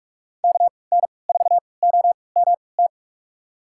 Morse code, heard as a keyed tone, about 44 wpm, spaced as for 25 wpm, 700 Hz. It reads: KN4OMT
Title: USA Call Signs - 6 Characters - 3x Character Spacing - 45wpm